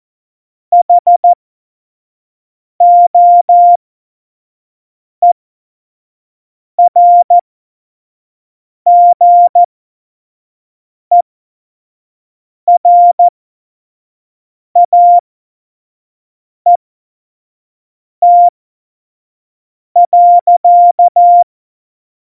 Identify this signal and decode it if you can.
Morse code, heard as a keyed tone, about 14 wpm, spaced as for 5 wpm, 700 Hz. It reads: HOERGERAET.
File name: Morsekurs / Witze / Witz 10. September 2019, 25 F70